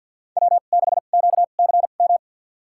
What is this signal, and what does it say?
Morse code, heard as a keyed tone, about 51 wpm, spaced as for 36 wpm, 700 Hz. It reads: W6QCM
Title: USA Call Signs - 5 Characters - 2x Character Spacing - 50wpm